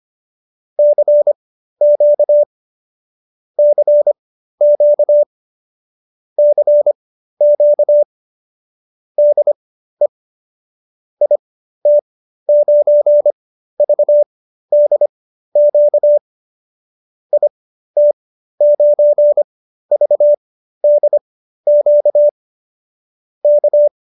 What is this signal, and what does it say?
Morse code, heard as a keyed tone, about 25 wpm, spaced as for 13 wpm, 590 Hz. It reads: CQ CQ CQ DE IT9VDQ IT9VDQ K
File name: qso215b 13